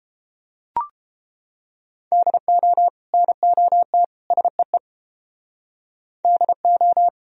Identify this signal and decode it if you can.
Morse code, heard as a keyed tone, about 30 wpm, 700 Hz. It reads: DO NOT SEE DO